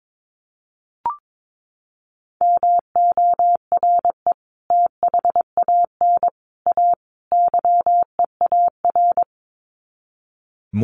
Morse code, heard as a keyed tone, about 22 wpm, 700 Hz. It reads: MORE THAN A YEAR
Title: Sets of 4 Words 22wpm